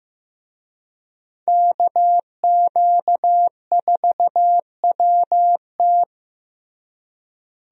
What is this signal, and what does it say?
Morse code, heard as a keyed tone, about 15 wpm, 700 Hz. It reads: KQ4WT
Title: USA Call Signs - 5 Characters - Repeated 3x - 15wpm